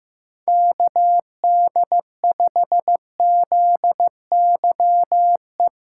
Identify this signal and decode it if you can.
Morse code, heard as a keyed tone, about 16 wpm, 700 Hz. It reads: KD5ZYE
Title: USA Callsign DE Callsign 15wpm